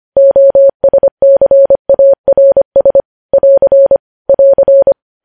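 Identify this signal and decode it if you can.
Morse code, heard as a keyed tone, about 25 wpm, 550 Hz. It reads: OSCARS <AR> <AR>
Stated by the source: Morse code practice of news headlines at 25 wpm, opening with vvv.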